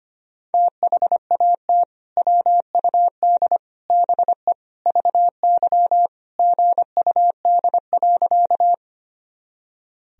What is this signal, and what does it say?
Morse code, heard as a keyed tone, about 25 wpm, 700 Hz. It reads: THAT WUD BE VY GUD.
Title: Sentences from Top 200 Words and Abbreviations 25wpm